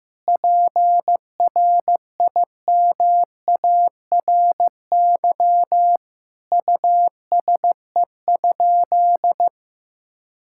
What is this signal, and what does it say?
Morse code, heard as a keyed tone, about 15 wpm, 700 Hz. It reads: PRIMARY USE?
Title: Sentences from Top 1000 Words - Review Part 5 - 15wpm